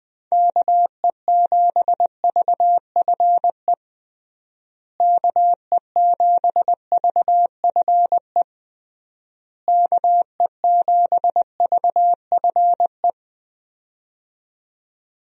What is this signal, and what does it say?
Morse code, heard as a keyed tone, about 20 wpm, 700 Hz. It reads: KE7VFE KE7VFE KE7VFE
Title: USA Call Signs - 6 Characters - Repeated 3x - 20wpm